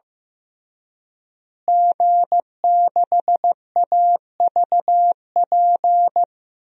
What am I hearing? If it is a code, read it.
Morse code, heard as a keyed tone, about 15 wpm, 700 Hz. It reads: G6AVP
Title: DXCC-100 Call Signs - 5 Characters - Repeated 3x - 15wpm